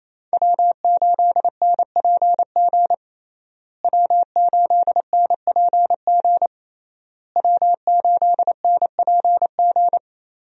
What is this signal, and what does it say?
Morse code, heard as a keyed tone, about 28 wpm, 700 Hz. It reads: W8NPG W8NPG W8NPG
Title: USA Call Signs - 5 Characters - Repeated 3x - 28wpm